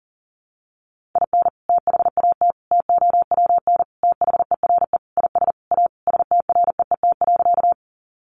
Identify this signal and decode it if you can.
Morse code, heard as a keyed tone, about 40 wpm, 700 Hz. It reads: IN THAT TOWN THERE IS A STREET.
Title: Sentences from Top 500 Words 40wpm